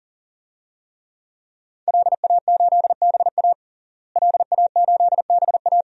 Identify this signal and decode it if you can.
Morse code, heard as a keyed tone, about 37 wpm, 700 Hz. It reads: LA8BA LA8BA
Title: DXCC-100 Call Signs - 5 Characters - Repeated 2x - 40wpm